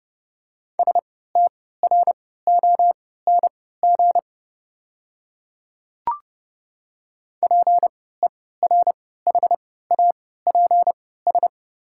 Morse code, heard as a keyed tone, about 30 wpm, spaced as for 17 wpm, 700 Hz. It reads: STRONG PERHAPS
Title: 3x Character Spacing - Top 500 Words - 30wpm